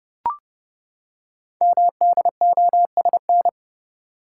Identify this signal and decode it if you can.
Morse code, heard as a keyed tone, about 29 wpm, 700 Hz. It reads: MDOSN
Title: Code Groups - 5 Letter:Numbers - 30wpm